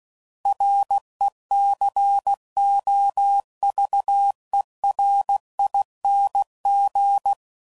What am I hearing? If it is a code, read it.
Morse code, heard as a keyed tone, about 16 wpm, 800 Hz. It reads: RECOVERING
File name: Suffixes / Special INGs / Sngl Ings